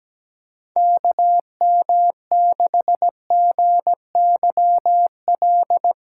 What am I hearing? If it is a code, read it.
Morse code, heard as a keyed tone, about 17 wpm, 700 Hz. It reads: KM6GYL
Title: USA Call Signs - 2x Character Spacing - 17wpm